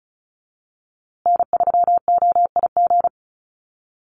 Morse code, heard as a keyed tone, about 34 wpm, 700 Hz. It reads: N3OIG
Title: USA Call Signs Repeated 2x 35wpm